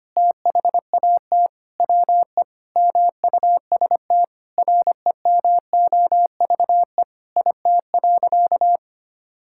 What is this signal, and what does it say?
Morse code, heard as a keyed tone, about 25 wpm, 700 Hz. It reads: THAT WE MUST REMOVE IT.
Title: Sentences from Top 1000 Words - Review Part 5 - 25wpm